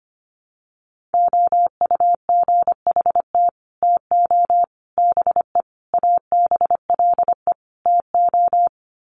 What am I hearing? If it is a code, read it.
Morse code, heard as a keyed tone, about 25 wpm, 700 Hz. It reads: OUGHT TO BE ABLE TO
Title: Sets of 5 Words Encore2 25wpm